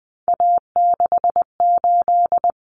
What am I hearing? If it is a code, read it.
Morse code, heard as a keyed tone, about 20 wpm, 700 Hz. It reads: A68
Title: Common Cut Numbers - 3-Digit 20wpm